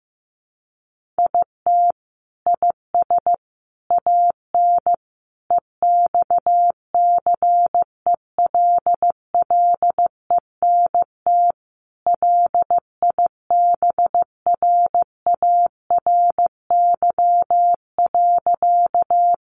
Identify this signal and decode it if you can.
Morse code, heard as a keyed tone, about 14 wpm, 700 Hz. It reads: IT IS AN EXCELLENT LIBRARY.
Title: Sentences from Top 1000 Words 15wpm